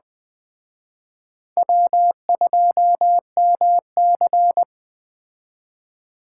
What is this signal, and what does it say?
Morse code, heard as a keyed tone, about 20 wpm, 700 Hz. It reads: W2MC